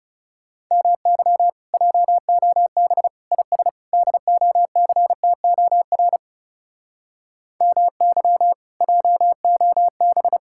Morse code, heard as a keyed tone, about 38 wpm, 700 Hz. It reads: MY JOB IS DOCTOR MY JOB
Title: Speed-Racing Common QSO Phrases - 30wpm